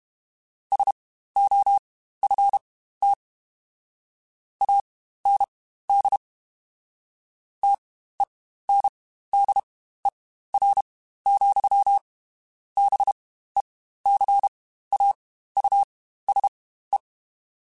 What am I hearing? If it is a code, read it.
Morse code, heard as a keyed tone, about 32 wpm, spaced as for 15 wpm, 800 Hz. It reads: SOFT AND TENDER, BECAUSE
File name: Spaced / Chuckles Spaced-mp3s / SoftWords sp @30